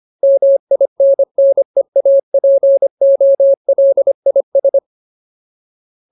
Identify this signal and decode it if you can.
Morse code, heard as a keyed tone, about 25 wpm, 550 Hz. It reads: MINNEAPOLIS
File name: Ultimate Word List / 25WPM Ultimate